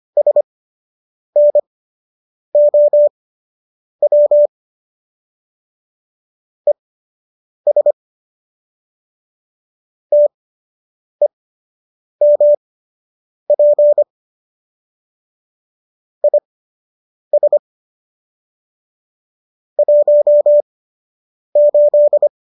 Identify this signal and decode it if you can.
Morse code, heard as a keyed tone, about 25 wpm, spaced as for 8 wpm, 590 Hz. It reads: SNOW ES TEMP IS 18